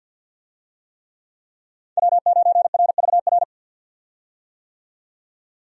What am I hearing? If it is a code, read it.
Morse code, heard as a keyed tone, about 51 wpm, 700 Hz. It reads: W9RVR
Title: USA Call Signs - 5 Characters - 50wpm